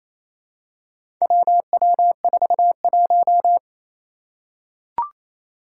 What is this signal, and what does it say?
Morse code, heard as a keyed tone, about 27 wpm, 700 Hz. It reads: WW41
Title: Four Challenging Letter-Number Combinations 28wpm